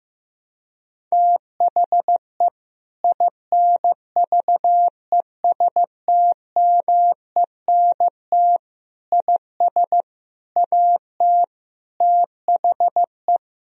Morse code, heard as a keyed tone, about 15 wpm, 700 Hz. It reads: THE INVESTMENT IS AT THE
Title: Sentences from Top 1000 Words - Review Part 4 - 15wpm